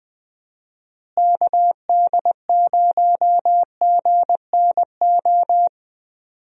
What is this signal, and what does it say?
Morse code, heard as a keyed tone, about 20 wpm, 700 Hz. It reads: KD0GNO
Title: USA Call Signs - 6 Characters - Repeated 2x - 20wpm